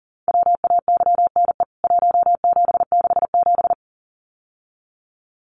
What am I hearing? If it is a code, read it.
Morse code, heard as a keyed tone, about 40 wpm, 700 Hz. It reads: WAYNE 1767
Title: CWT Exchange 40wpm